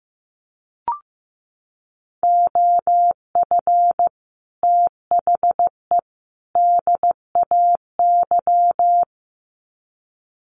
Morse code, heard as a keyed tone, about 15 wpm, 700 Hz. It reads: OF THE DAY
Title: Sets of 3 Words 15wpm